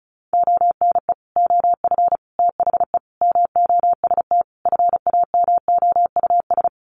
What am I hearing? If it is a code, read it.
Morse code, heard as a keyed tone, about 34 wpm, 700 Hz. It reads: ONE OF THE MOST FAMOUS
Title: Sets of 5 Words Encore2 35wpm